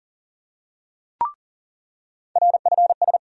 Morse code, heard as a keyed tone, about 40 wpm, 700 Hz. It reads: RFS
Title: USA Call Sign Postfixes - 3 Letters 40wpm